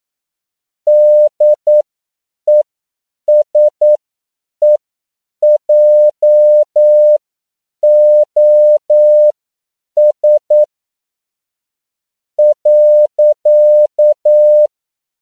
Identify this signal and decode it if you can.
Morse code, heard as a keyed tone, about 9 wpm, spaced as for 7 wpm, 600 Hz. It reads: DESEJOS .
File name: CW / 10PPM